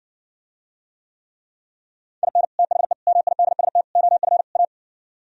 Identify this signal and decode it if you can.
Morse code, heard as a keyed tone, about 60 wpm, 700 Hz. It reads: IN THE MIDST OF A